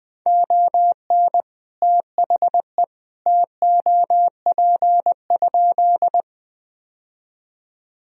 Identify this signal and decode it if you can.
Morse code, heard as a keyed tone, about 20 wpm, 700 Hz. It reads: ON THE TOP?
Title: Sentences from Top 1000 Words - Review Part 2 - 20wpm